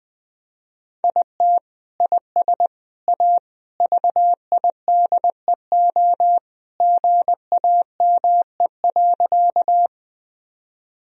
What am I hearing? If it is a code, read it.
Morse code, heard as a keyed tone, about 20 wpm, 700 Hz. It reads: IT IS A VIDEO GAME.
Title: Sentences from Top 1000 Words - Review Part 4 - 20wpm